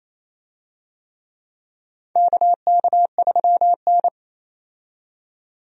Morse code, heard as a keyed tone, about 28 wpm, 700 Hz. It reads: KK3N
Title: USA Call Signs - 4 Characters - 28wpm